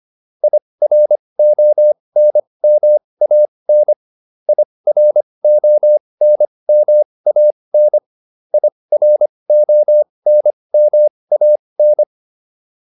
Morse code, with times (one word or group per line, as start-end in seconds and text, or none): 0.4-3.9 IRONMAN
4.5-8.0 IRONMAN
8.5-12.0 IRONMAN